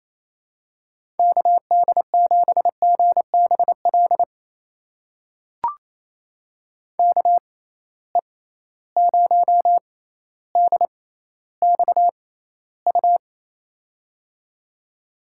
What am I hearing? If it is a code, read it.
Morse code, heard as a keyed tone, about 28 wpm, spaced as for 14 wpm, 700 Hz. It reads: KD7GBL KE0DXU
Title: USA Call Signs - 6 Characters - 6x Character Spacing - 28wpm